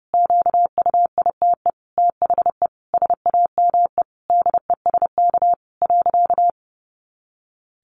Morse code, heard as a keyed tone, about 30 wpm, 700 Hz. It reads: QUITE THE SAME DESK.
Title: A Sound Of Thunder 30wpm